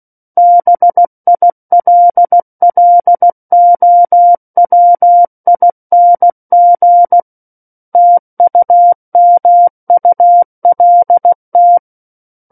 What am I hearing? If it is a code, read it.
Morse code, heard as a keyed tone, about 16 wpm, 700 Hz. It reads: BILLOWING TUMULT
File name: Book One / 07-Chapter 7 - 16 WPM